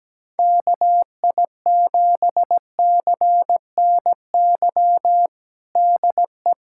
Morse code, heard as a keyed tone, about 17 wpm, 700 Hz. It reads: KI7CNY DE
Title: USA Callsign DE Callsign 17wpm